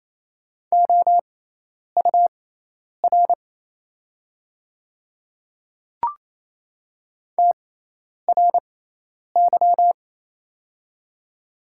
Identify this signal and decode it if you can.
Morse code, heard as a keyed tone, about 27 wpm, spaced as for 10 wpm, 700 Hz. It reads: OUR TRY
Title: 6x Character Spacing - Top 300 Words - 28wpm